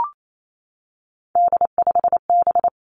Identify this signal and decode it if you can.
Morse code, heard as a keyed tone, about 28 wpm, 700 Hz. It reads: D5B
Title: Three Challenging Letter-Number Combinations 28wpm